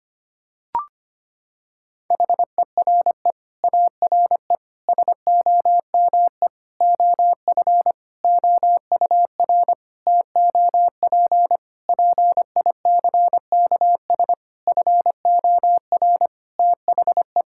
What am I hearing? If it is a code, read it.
Morse code, heard as a keyed tone, about 25 wpm, 700 Hz. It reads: HERE ARE SOME OF OUR TOP PICKS FOR THE
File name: practice / 25wpm / Sentences with 9-Word Phrases - 25wpm